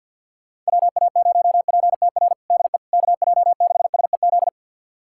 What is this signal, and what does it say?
Morse code, heard as a keyed tone, about 52 wpm, 690 Hz. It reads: WA0PTR DE KJ6SEZ
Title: USA Callsign DE Callsign - 6 Characters - 50wpm